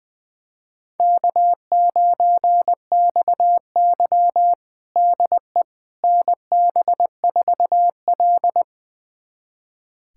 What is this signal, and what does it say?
Morse code, heard as a keyed tone, about 20 wpm, 700 Hz. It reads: K9XY DE NB4L
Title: USA Callsign DE Callsign - 4 Characters - 20wpm